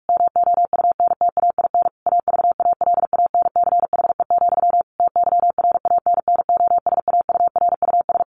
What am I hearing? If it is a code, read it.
Morse code, heard as a keyed tone, about 44 wpm, 700 Hz. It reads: MOUNTAIN AVALANCHE, TYRANNOSAURUS
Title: A Sound Of Thunder 45wpm